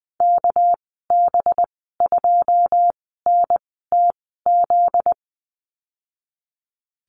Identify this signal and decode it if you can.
Morse code, heard as a keyed tone, about 20 wpm, spaced as for 15 wpm, 700 Hz. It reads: KB2NTZ
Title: 245 Lesson 39g - Letter z - Call Signs 20wpm